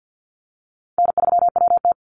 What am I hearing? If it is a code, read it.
Morse code, heard as a keyed tone, about 50 wpm, 700 Hz. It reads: N3WT
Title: Easy Call Signs - 50wpm